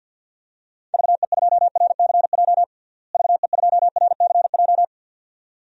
Morse code, heard as a keyed tone, about 49 wpm, 700 Hz. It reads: VE2RCJ VE2RCJ
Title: DXCC-100 Call Signs - 6 Characters - Repeated 2x - 50wpm